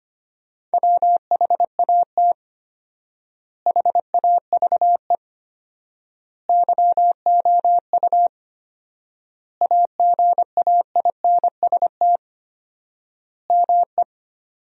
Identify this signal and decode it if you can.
Morse code, heard as a keyed tone, about 25 wpm, 700 Hz. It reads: WHAT HAVE YOU AGAINST ME
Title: 4x Word-Spacing - Sentences from Top 400 Words - 25wpm